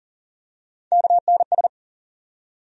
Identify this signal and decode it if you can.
Morse code, heard as a keyed tone, about 39 wpm, 700 Hz. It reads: KNS
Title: USA Call Sign Postfixes - 3 Letters 40wpm